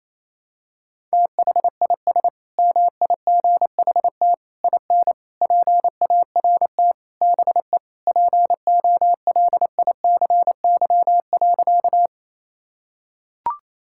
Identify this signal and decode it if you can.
Morse code, heard as a keyed tone, about 28 wpm, 700 Hz. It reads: THIS MIGHT IN PART BE POLICY.
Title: Sentences from Top 1000 Words - Review Part 5 - 28wpm